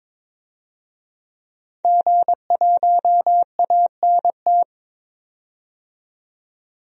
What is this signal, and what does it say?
Morse code, heard as a keyed tone, about 21 wpm, 700 Hz. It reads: G1ANT
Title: DXCC-100 Call Signs - 5 Characters - 22wpm